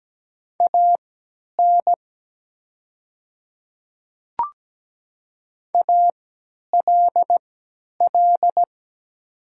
Morse code, heard as a keyed tone, about 17 wpm, spaced as for 10 wpm, 700 Hz. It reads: AN ALL
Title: 3x Character Spacing - Top 100 Words - 17wpm